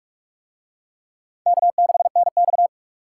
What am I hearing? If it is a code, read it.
Morse code, heard as a keyed tone, about 45 wpm, 700 Hz. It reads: K6NX